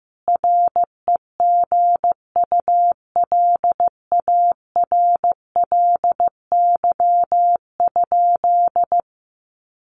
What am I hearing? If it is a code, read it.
Morse code, heard as a keyed tone, about 14 wpm, 700 Hz. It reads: REGULARLY?
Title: Sentences from Top 1400 Words 15wpm